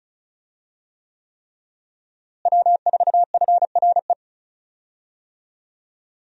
Morse code, heard as a keyed tone, about 34 wpm, 700 Hz. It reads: W4FRE
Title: USA Call Signs - 5 Characters - Repeated 3x - 35wpm